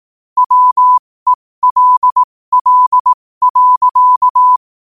Morse code, heard as a keyed tone, about 18 wpm, spaced as for 15 wpm, 1000 Hz. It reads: WELL.
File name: qotdfiles / 15wpm